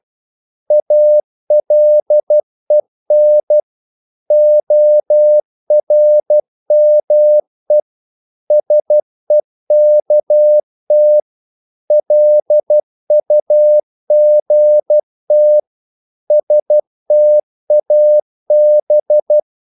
Morse code, heard as a keyed tone, about 12 wpm, 600 Hz. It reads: ALEN ORME SEKT LUGT STAB
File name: Korte ord DK 12wpm